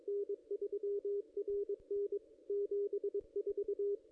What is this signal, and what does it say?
Morse code, heard as a keyed tone, about 22 wpm, 440 Hz, with radio noise. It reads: N3RN 74
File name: contests / ss-cw-k5tr-op-n5rz -1600